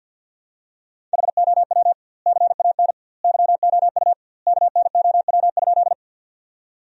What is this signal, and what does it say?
Morse code, heard as a keyed tone, about 52 wpm, 700 Hz. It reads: HOW CAN YOU KNOW?
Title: 2x Word-Spacing - Sentences from Top 100 Words - 50wpm